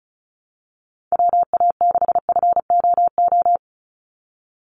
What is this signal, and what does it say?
Morse code, heard as a keyed tone, about 36 wpm, 700 Hz. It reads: WA6FOO
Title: USA Call Signs 35wpm